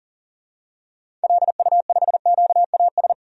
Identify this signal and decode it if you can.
Morse code, heard as a keyed tone, about 40 wpm, 700 Hz. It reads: LU5QAS
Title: DXCC-100 Call Signs - 6 Characters - 40wpm